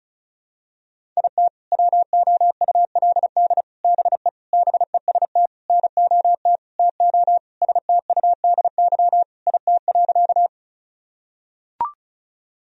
Morse code, heard as a keyed tone, about 34 wpm, 700 Hz. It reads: IT WOULD BE BEST NOT TO STUDY IT.
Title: Sentences from Top 1000 Words - Review Part 1 - 35wpm